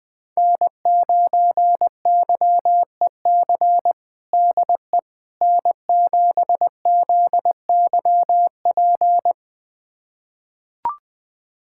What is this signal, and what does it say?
Morse code, heard as a keyed tone, about 20 wpm, 700 Hz. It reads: N9YEC DE N7ZYP
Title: USA Callsign DE Callsign - 5 Characters - 20wpm